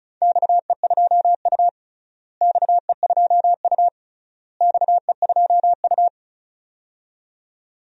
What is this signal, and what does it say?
Morse code, heard as a keyed tone, about 36 wpm, 700 Hz. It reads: XE2U XE2U XE2U
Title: DXCC-100 Call Signs - 4 Characters - Repeated 3x - 35wpm